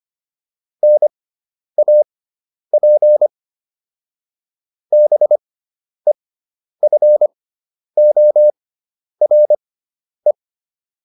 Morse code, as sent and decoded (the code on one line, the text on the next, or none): -. .- .--. / -... ..-. --- .-. .
NAP BEFORE